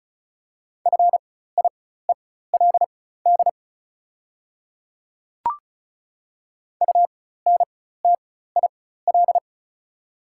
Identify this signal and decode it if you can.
Morse code, heard as a keyed tone, about 36 wpm, spaced as for 16 wpm, 700 Hz. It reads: FIELD UNTIL
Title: 4x Character Spacing - Top 500 Words - 35wpm